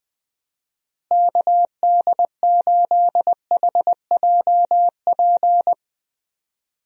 Morse code, heard as a keyed tone, about 21 wpm, 700 Hz. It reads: KD8HJP
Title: USA Call Signs - 6 Characters - Repeated 2x - 20wpm